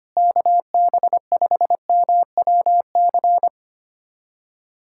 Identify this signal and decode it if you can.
Morse code, heard as a keyed tone, about 25 wpm, 700 Hz. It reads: KB5MWC
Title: USA Callsign DE Callsign - 6 Characters - 25wpm